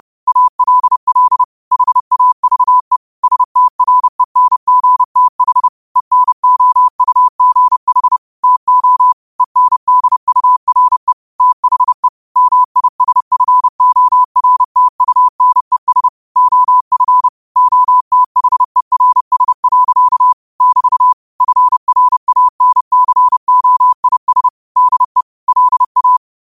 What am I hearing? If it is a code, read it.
Morse code, heard as a keyed tone, about 30 wpm, 1000 Hz. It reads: ALL HAVE STRENGTH ENOUGH TO ENDURE THE MISFORTUNES OF OTHERS. <BT> FRANCOIS DE LA
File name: qotdfiles / 30wpm